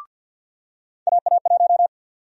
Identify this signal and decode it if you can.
Morse code, heard as a keyed tone, about 51 wpm, 700 Hz. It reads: AA1
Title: USA Call Sign Prefixes - 50wpm